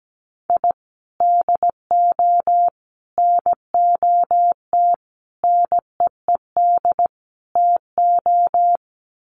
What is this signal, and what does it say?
Morse code, heard as a keyed tone, about 17 wpm, 700 Hz. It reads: I DO NOT NEED TO
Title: Sets of 5 Words 17wpm